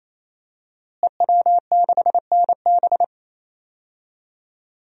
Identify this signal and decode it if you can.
Morse code, heard as a keyed tone, about 28 wpm, 700 Hz. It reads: EW6NB
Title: DXCC-100 Call Signs - 5 Characters - Repeated 3x - 28wpm